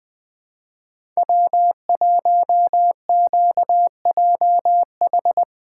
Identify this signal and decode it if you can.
Morse code, heard as a keyed tone, about 20 wpm, 700 Hz. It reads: W1QJH
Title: USA Callsign DE Callsign 20wpm